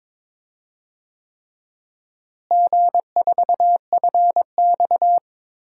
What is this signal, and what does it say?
Morse code, heard as a keyed tone, about 22 wpm, 700 Hz. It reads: G4FX
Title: DXCC-100 Call Signs - 4 Characters - Repeated 3x - 22wpm